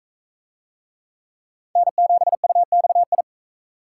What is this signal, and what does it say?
Morse code, heard as a keyed tone, about 39 wpm, 700 Hz. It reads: N7UXI